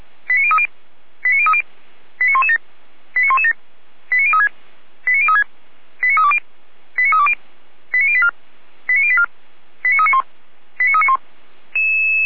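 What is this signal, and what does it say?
Five-tone selective call (ZVEI), heard as quick runs of five tones, tones of about 70 ms. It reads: 89030 89030 89117 89117 89036 89036 89320 89320 89084 89084 89391 89391
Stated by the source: Five-tone ZVEI selective call received by SDR, from feuerwehr-orlamuende.org fire brigade archive.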